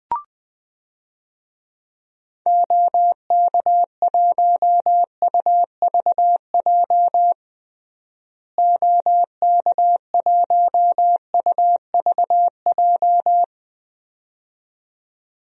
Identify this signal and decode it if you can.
Morse code, heard as a keyed tone, about 21 wpm, 700 Hz. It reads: OK1UVJ OK1UVJ
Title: DXCC-100 Call Signs - 6 Characters - Repeated 2x - 20wpm